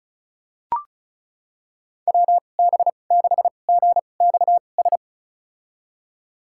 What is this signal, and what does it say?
Morse code, heard as a keyed tone, about 34 wpm, spaced as for 25 wpm, 700 Hz. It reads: WB6GXS
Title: USA Call Signs - 6 Characters - 2x Character Spacing - 35wpm